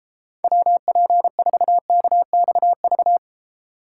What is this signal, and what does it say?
Morse code, heard as a keyed tone, about 29 wpm, 700 Hz. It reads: WP4KXV